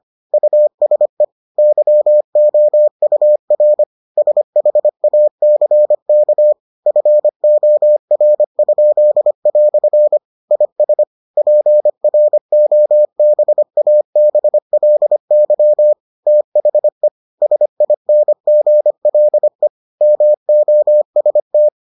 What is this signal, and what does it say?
Morse code, heard as a keyed tone, about 25 wpm, 600 Hz. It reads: USE YOUR SHACK FOR?" IS PROBABLY THE SINGLE MOST